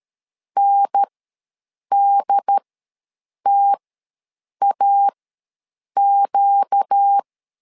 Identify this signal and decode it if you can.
Morse code, heard as a keyed tone, about 13 wpm, spaced as for 7 wpm, 800 Hz. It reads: NDTAQ
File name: LEZIONI / 4 Q1 ws 12wpm 2129